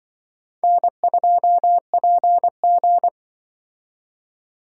Morse code, heard as a keyed tone, about 24 wpm, 700 Hz. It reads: N2PG
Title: Speed Racing - USA Call Signs 20wpm